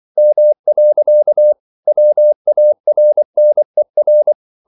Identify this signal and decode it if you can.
Morse code, heard as a keyed tone, about 24 wpm, 600 Hz. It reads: M. WARNER
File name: CW / Practice / EnglishQuotations0424